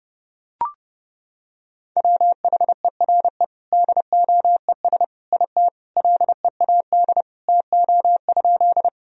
Morse code, heard as a keyed tone, about 32 wpm, 700 Hz. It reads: WHERE DOES IT LEAD TO?
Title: Sentences from Top 1000 Words - Review Part 2 - 30wpm